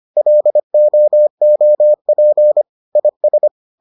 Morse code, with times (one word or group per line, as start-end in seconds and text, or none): 0.2-2.6 LOOP
3.0-3.5 IS